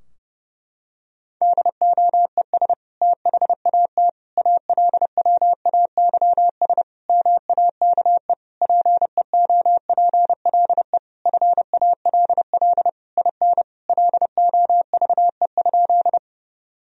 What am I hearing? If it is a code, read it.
Morse code, heard as a keyed tone, about 32 wpm, 700 Hz. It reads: DOES THAT ALWAYS MAKE PEOPLE FALL IN LOVE?
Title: Sentences from Top 1000 Words - Review Part 2 - 30wpm